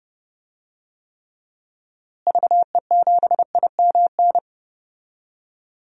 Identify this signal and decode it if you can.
Morse code, heard as a keyed tone, about 31 wpm, 700 Hz. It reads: VE7IMN
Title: DXCC-100 Call Signs - 6 Characters - Repeated 2x - 30wpm